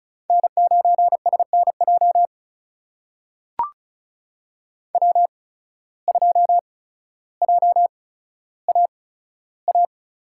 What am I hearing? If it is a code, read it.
Morse code, heard as a keyed tone, about 36 wpm, spaced as for 15 wpm, 700 Hz. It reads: N9SNJ W2JAA
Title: USA Call Signs - 8x Character Spacing - 35wpm